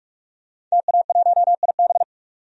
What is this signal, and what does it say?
Morse code, heard as a keyed tone, about 44 wpm, 690 Hz. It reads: TA1IB
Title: DXCC-100 Call Signs - 5 Characters - 45wpm